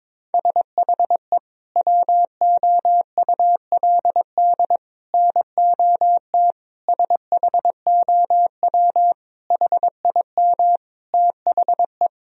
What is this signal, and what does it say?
Morse code, heard as a keyed tone, about 22 wpm, 700 Hz. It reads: SHE WOULD NOT SHOW HIM THE
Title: Sentences from Top 1000 Words - Review Part 3 - 22wpm